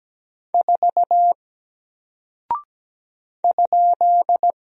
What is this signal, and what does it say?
Morse code, heard as a keyed tone, about 17 wpm, 700 Hz. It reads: 4 ?
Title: Single Character (Letter:Number:Punctuation:Prosign) 17wpm